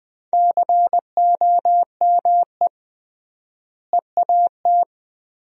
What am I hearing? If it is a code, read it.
Morse code, heard as a keyed tone, about 20 wpm, 700 Hz. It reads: COME EAT
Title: 3x Word-Spacing - Sentences from Top 300 Words - 20wpm